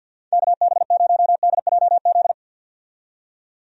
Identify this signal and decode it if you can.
Morse code, heard as a keyed tone, about 48 wpm, 700 Hz. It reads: KB0DJZ
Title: USA Call Signs - 6 Characters - 50wpm